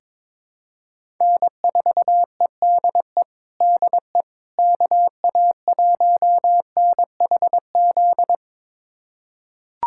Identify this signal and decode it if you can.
Morse code, heard as a keyed tone, about 23 wpm, 700 Hz. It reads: N4EDE DE KA1NHZ
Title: USA Callsign DE Callsign 22wpm